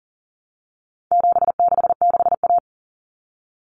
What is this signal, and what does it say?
Morse code, heard as a keyed tone, about 40 wpm, 700 Hz. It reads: 766A
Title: Common Cut Numbers - 4-Digit 40wpm